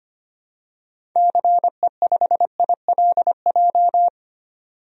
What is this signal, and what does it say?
Morse code, heard as a keyed tone, about 25 wpm, 700 Hz. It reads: CE5ILJ
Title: DXCC-100 Call Signs - 6 Characters - 25wpm